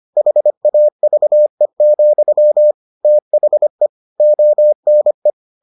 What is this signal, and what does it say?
Morse code, heard as a keyed tone, about 25 wpm, 600 Hz. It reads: HAVE, THE ONE